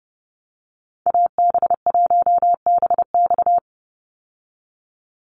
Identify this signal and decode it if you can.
Morse code, heard as a keyed tone, about 30 wpm, 700 Hz. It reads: AB1BX
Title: Call Signs Repeated 3x - 30wpm